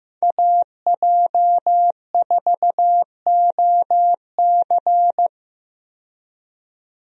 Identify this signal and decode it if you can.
Morse code, heard as a keyed tone, about 15 wpm, 700 Hz. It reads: AJ4OC
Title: USA Call Signs - 5 Characters - 15wpm